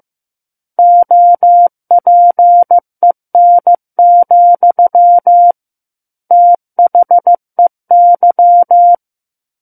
Morse code, heard as a keyed tone, about 15 wpm, 700 Hz. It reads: OPEN, THEY
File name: Book One / 06-Chapter 6 - 15 WPM